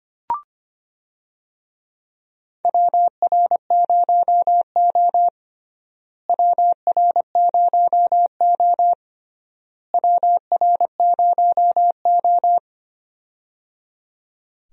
Morse code, heard as a keyed tone, about 25 wpm, 700 Hz. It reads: WR0O WR0O WR0O
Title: USA Call Signs - 4 Characters - Repeated 3x - 25wpm